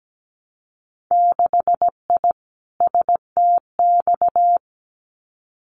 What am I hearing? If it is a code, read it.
Morse code, heard as a keyed tone, about 17 wpm, 700 Hz. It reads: 6I STX
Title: Winter Field Day Exchanges Encore 17wpm